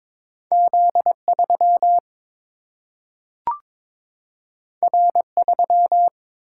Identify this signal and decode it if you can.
Morse code, heard as a keyed tone, about 22 wpm, 700 Hz. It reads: Z 3 R 3
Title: Single-Letter Single-Number 22wpm